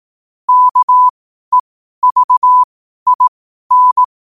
Morse code, heard as a keyed tone, about 18 wpm, spaced as for 12 wpm, 1000 Hz. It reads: KEVIN
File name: qotdfiles / 12wpm